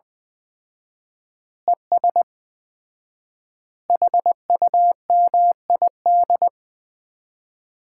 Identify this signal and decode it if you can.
Morse code, heard as a keyed tone, about 20 wpm, 700 Hz. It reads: ES HUMID
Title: Common QSO Phrases - 4x Word-Spacing - 20wpm